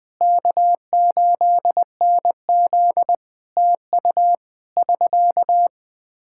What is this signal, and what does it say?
Morse code, heard as a keyed tone, about 20 wpm, 700 Hz. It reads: K8NZ TU <SK>
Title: Practice QSO - Exchange 4 - Ending - 20wpm